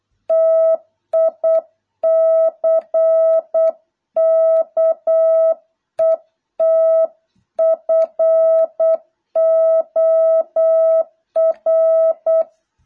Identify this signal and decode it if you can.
Morse code, heard as a keyed tone, about 8 wpm, 650 Hz. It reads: TICKET FOR